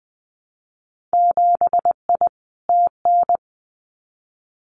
Morse code, heard as a keyed tone, about 20 wpm, 700 Hz. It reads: 7I TN
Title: Winter Field Day Exchanges Encore 20wpm